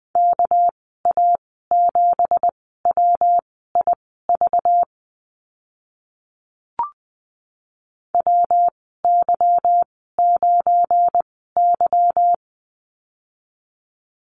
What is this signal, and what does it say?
Morse code, heard as a keyed tone, about 19 wpm, spaced as for 14 wpm, 700 Hz. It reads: KA7WIV WY9Y
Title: 200 Lesson 32g - Letter k - Call Signs 20wpm